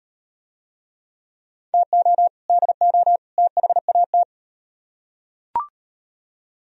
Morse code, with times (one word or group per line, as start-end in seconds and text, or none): 1.7-2.3 TO
2.5-3.2 DO
3.4-4.2 THAT